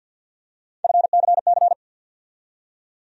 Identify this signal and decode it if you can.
Morse code, heard as a keyed tone, about 50 wpm, 700 Hz. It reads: FCC
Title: USA Call Sign Postfixes - 3 Letters 50wpm